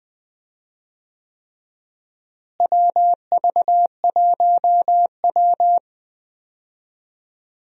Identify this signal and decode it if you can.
Morse code, heard as a keyed tone, about 20 wpm, 700 Hz. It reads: WV1W
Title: USA Call Signs - 4 Characters - Repeated 3x - 20wpm